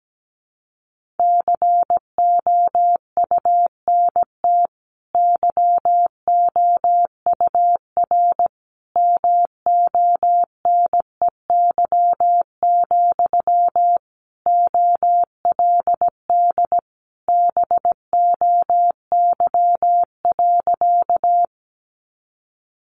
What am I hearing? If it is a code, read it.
Morse code, heard as a keyed tone, about 17 wpm, 700 Hz. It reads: COUNT YOUR MONEY, OLD BOY.
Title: Sentences from Top 1500 Words 17wpm